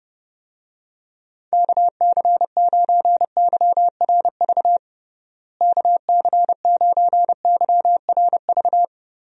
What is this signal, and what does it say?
Morse code, heard as a keyed tone, about 30 wpm, 700 Hz. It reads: KC9YRV KC9YRV
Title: USA Call Signs - 6 Characters - Repeated 2x - 30wpm